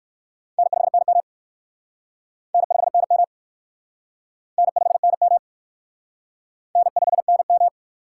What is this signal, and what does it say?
Morse code, heard as a keyed tone, about 54 wpm, 700 Hz. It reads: N5NM N5NM N5NM N5NM
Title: Speed Racing - USA 1x2 Call Signs 45wpm